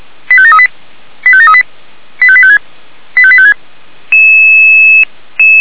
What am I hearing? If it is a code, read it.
Five-tone selective call (ZVEI), heard as quick runs of five tones, tones of about 70 ms. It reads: 85738 85738 85856 85856
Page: /464/SDR/ZVEI/85855/